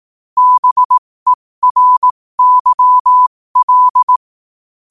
Morse code, heard as a keyed tone, about 18 wpm, spaced as for 15 wpm, 1000 Hz. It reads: BERYL
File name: qotdfiles / 15wpm